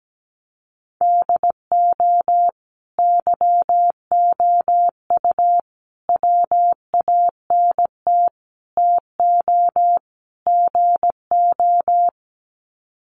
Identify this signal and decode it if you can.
Morse code, heard as a keyed tone, about 17 wpm, 700 Hz. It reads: DO YOU WANT TO GO
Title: Sets of 5 Words Encore2 17wpm